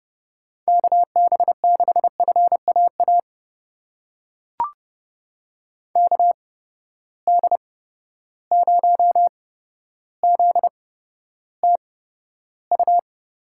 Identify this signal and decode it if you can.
Morse code, heard as a keyed tone, about 31 wpm, spaced as for 13 wpm, 700 Hz. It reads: KB6FAA KD0ZTU
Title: USA Call Signs - 6 Characters - 8x Character Spacing - 30wpm